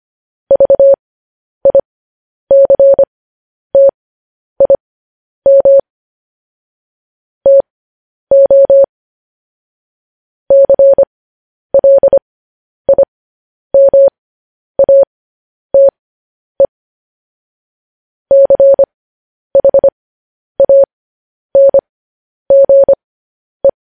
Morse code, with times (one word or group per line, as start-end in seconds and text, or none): 0.5-5.8 VICTIM
7.5-8.8 TO
10.5-16.7 CLIMATE
18.3-23.7 CHANGE